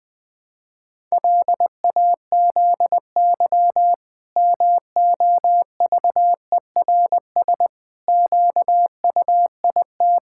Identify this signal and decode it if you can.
Morse code, heard as a keyed tone, about 21 wpm, 700 Hz. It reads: LAZY MOVERS QUIT
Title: Pangram Challenge v2 - 20wpm